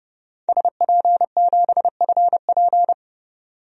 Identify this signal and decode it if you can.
Morse code, heard as a keyed tone, about 30 wpm, 700 Hz. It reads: SP7FP
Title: DXCC-100 Call Signs - 5 Characters - 30wpm